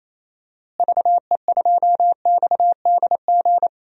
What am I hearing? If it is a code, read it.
Morse code, heard as a keyed tone, about 28 wpm, 700 Hz. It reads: VE2XDG
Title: DXCC-100 Call Signs - 6 Characters - Repeated 3x - 28wpm